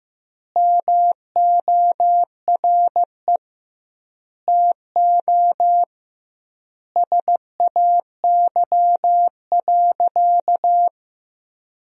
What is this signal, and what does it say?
Morse code, heard as a keyed tone, about 15 wpm, 700 Hz. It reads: MORE TO SAY.
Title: 2x Word-Spacing - Sentences from Top 500 Words - 15wpm